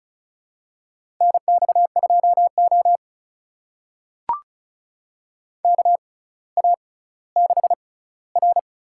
Morse code, heard as a keyed tone, about 36 wpm, spaced as for 18 wpm, 700 Hz. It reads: NX2O KA6R
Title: USA Call Signs - 4 Characters - 6x Character Spacing - 35wpm